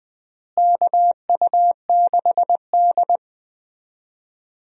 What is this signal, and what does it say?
Morse code, heard as a keyed tone, about 20 wpm, 700 Hz. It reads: KU6D
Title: USA Call Signs - 4 Characters - Repeated 3x - 20wpm